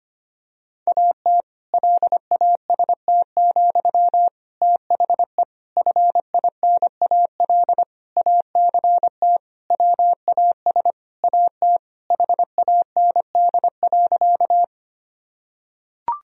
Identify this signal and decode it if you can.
Morse code, heard as a keyed tone, about 25 wpm, 700 Hz. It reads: AT LAST, THE FINAL ACT WAS AT HAND.
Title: Sentences from Top 1000 Words - Review Part 2 - 25wpm